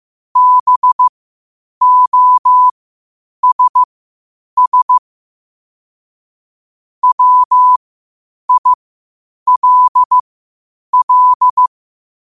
Morse code, heard as a keyed tone, about 15 wpm, spaced as for 8 wpm, 1000 Hz. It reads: BOSS WILL